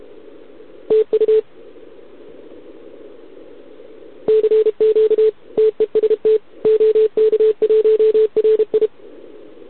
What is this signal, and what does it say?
Morse code, heard as a keyed tone, about 32 wpm, 440 Hz. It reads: TU CQ TEST OK1RI
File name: ARCHIV / OKDX2020 / OK1RIb